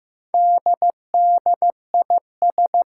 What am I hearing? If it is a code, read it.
Morse code, heard as a keyed tone, about 15 wpm, 700 Hz. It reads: DDIS
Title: Four Challenging Letter-Number Combinations 15wpm